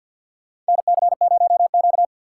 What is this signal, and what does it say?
Morse code, heard as a keyed tone, about 52 wpm, 700 Hz. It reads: NC0Q
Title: USA Call Signs - 3x Character Spacing - 50wpm